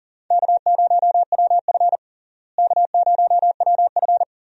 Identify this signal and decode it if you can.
Morse code, heard as a keyed tone, about 40 wpm, 700 Hz. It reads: K0WF K0WF
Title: USA Call Signs - 4 Characters - Repeated 3x - 40wpm